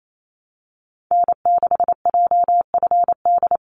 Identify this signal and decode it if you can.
Morse code, heard as a keyed tone, about 28 wpm, 700 Hz. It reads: N6JFD
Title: NA SOTA Activator Callsigns 28wpm